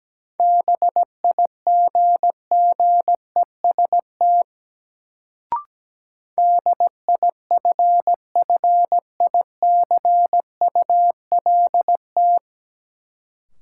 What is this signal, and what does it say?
Morse code, heard as a keyed tone, about 17 wpm, 700 Hz. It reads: BIGGEST DIFFICULT
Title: Sets of 4 Words - New Word Drill - 17wpm